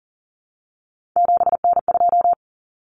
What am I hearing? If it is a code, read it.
Morse code, heard as a keyed tone, about 40 wpm, 700 Hz. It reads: 7N2
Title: Common Cut Numbers - 3-Digit 40wpm